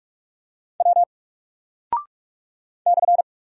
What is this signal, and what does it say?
Morse code, heard as a keyed tone, about 44 wpm, 700 Hz. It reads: W /
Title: Single Character (Letter:Number:Punctuation:Prosign) Alt 45wpm